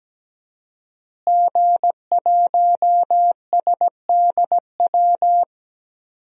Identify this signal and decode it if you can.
Morse code, heard as a keyed tone, about 17 wpm, 700 Hz. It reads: G1SDW
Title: DXCC-100 Call Signs - 5 Characters - Repeated 3x - 17wpm